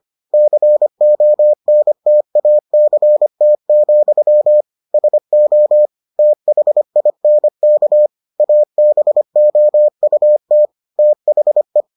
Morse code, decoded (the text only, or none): CONTACT, SO THINK ABOUT THE